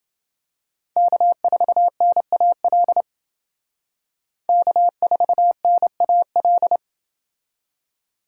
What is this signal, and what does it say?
Morse code, heard as a keyed tone, about 28 wpm, 700 Hz. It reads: K4NAL K4NAL